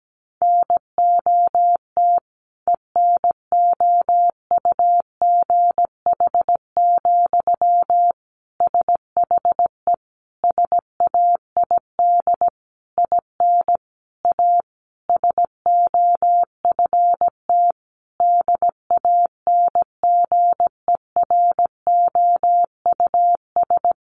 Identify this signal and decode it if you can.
Morse code, heard as a keyed tone, about 17 wpm, 700 Hz. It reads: NOT ENOUGH, SHE SAID IN A SOFT DANGEROUS
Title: Sentences from Top 1500 Words 17wpm